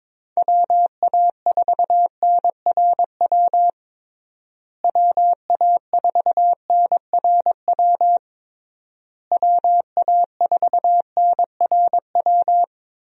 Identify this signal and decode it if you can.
Morse code, heard as a keyed tone, about 23 wpm, 700 Hz. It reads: WA4NRW WA4NRW WA4NRW
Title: USA Call Signs - 6 Characters - Repeated 3x - 22wpm